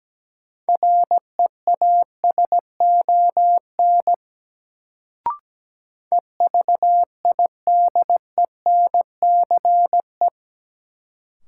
Morse code, heard as a keyed tone, about 17 wpm, 700 Hz. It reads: REASON EVIDENCE
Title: Sets of 5 Words Encore - New Word Drill - 17wpm